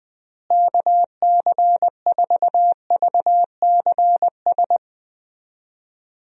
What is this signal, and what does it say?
Morse code, heard as a keyed tone, about 20 wpm, 700 Hz. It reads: KC4VCS
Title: USA Call Signs - 6 Characters - Repeated 2x - 20wpm